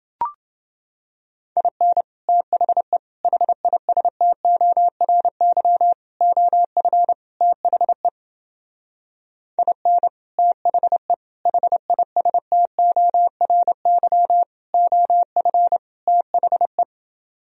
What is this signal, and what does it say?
Morse code, heard as a keyed tone, about 29 wpm, 700 Hz. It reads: IN THE HISTORY OF THE IN THE HISTORY OF THE